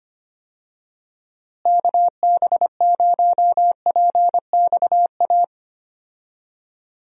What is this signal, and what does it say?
Morse code, heard as a keyed tone, about 25 wpm, 700 Hz. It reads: KB0PXA
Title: USA Call Signs - 6 Characters - Repeated 3x - 25wpm